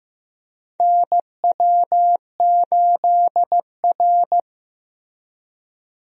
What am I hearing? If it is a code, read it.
Morse code, heard as a keyed tone, about 15 wpm, 700 Hz. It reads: NW8R
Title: USA Call Signs - 4 Characters - Repeated 2x - 15wpm